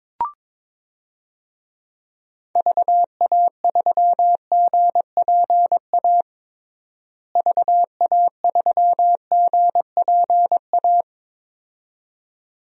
Morse code, heard as a keyed tone, about 22 wpm, 700 Hz. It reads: VA3GPA VA3GPA
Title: DXCC-100 Call Signs - 6 Characters - Repeated 2x - 22wpm